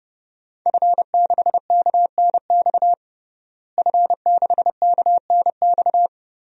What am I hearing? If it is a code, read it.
Morse code, heard as a keyed tone, about 30 wpm, 700 Hz. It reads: F6KNX F6KNX
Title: DXCC-100 Call Signs - 5 Characters - Repeated 3x - 30wpm